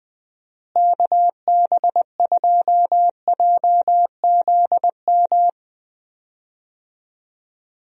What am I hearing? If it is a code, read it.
Morse code, heard as a keyed tone, about 20 wpm, 700 Hz. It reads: KB2JZM
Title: USA Call Signs - 6 Characters - Repeated 3x - 20wpm